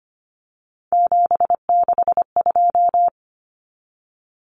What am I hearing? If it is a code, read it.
Morse code, heard as a keyed tone, about 25 wpm, 700 Hz. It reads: 762
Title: Common Cut Numbers - 3-Digit 25wpm